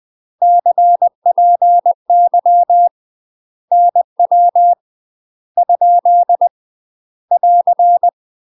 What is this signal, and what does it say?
Morse code, heard as a keyed tone, about 20 wpm, 700 Hz. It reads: CPY NW ? <AR>
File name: UK MASTER QSOs TRAINING / 20WPM / UK MASTER QSOs 0068